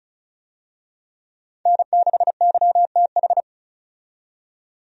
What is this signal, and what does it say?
Morse code, heard as a keyed tone, about 34 wpm, 700 Hz. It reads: N6YTH
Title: USA Call Signs - 5 Characters - 35wpm